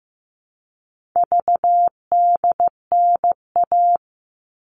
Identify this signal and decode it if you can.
Morse code, heard as a keyed tone, about 15 wpm, 700 Hz. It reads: VDNA
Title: All Cut Numbers - 4-Digit 15wpm